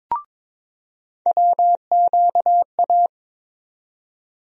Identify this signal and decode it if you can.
Morse code, heard as a keyed tone, about 22 wpm, 700 Hz. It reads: WQA
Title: USA Call Sign Postfixes - 3 Letters 22wpm